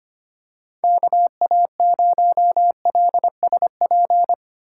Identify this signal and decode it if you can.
Morse code, heard as a keyed tone, about 25 wpm, 700 Hz. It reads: KA0LSP
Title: USA Call Signs - 6 Characters - 3x Character Spacing - 25wpm